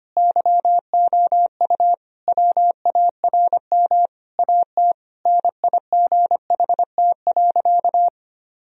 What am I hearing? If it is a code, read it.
Morse code, heard as a keyed tone, about 25 wpm, 700 Hz. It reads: YOU WARM AT NIGHT.
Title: Sentences from Top 1000 Words - Review Part 2 - 25wpm